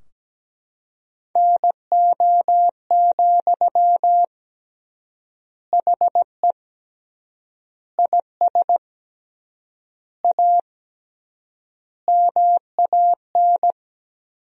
Morse code, heard as a keyed tone, about 17 wpm, 700 Hz. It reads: NO, HE IS A MAN
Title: 3x Word-Spacing - Sentences from Top 400 Words - 17wpm